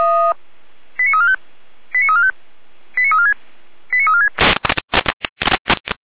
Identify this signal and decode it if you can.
Five-tone selective call (ZVEI), heard as quick runs of five tones, tones of about 70 ms. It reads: 89345 89345 89346 89346